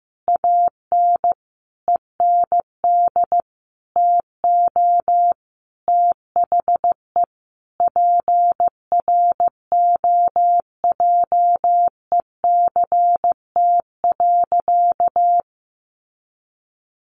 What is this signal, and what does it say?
Morse code, heard as a keyed tone, about 14 wpm, 700 Hz. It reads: AN END TO THE PROJECT.
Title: Sentences from Top 1400 Words 15wpm